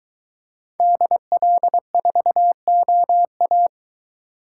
Morse code, heard as a keyed tone, about 22 wpm, 700 Hz. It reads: DL4OA